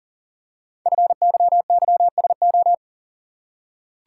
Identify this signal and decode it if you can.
Morse code, heard as a keyed tone, about 41 wpm, 700 Hz. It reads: FYYSO
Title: Code Groups - 5 Letters - 40wpm